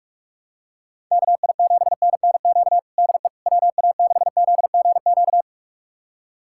Morse code, heard as a keyed tone, about 42 wpm, 700 Hz. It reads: KI7NNQ DE WA6ZGQ
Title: USA Callsign DE Callsign - 6 Characters - 45wpm